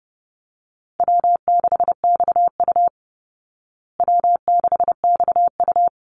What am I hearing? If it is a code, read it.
Morse code, heard as a keyed tone, about 30 wpm, 700 Hz. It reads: W6XU W6XU
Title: Easy Call Signs Repeated 2x - 30wpm